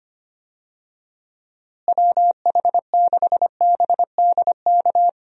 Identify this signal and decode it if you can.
Morse code, heard as a keyed tone, about 25 wpm, 700 Hz. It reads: WH6BDK
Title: USA Call Signs - 6 Characters - Repeated 2x - 25wpm